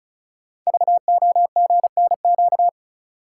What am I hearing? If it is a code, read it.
Morse code, heard as a keyed tone, about 36 wpm, 690 Hz. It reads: VOGNQ
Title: Code Groups - 5 Letters - 35wpm